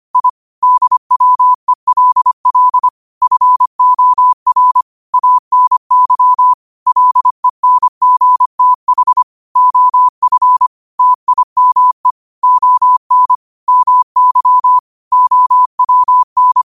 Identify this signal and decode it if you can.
Morse code, heard as a keyed tone, about 25 wpm, 1000 Hz. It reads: I DWELL FOR ANY LENGTH OF TIME ON MY OWN
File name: qotdfiles / 25wpm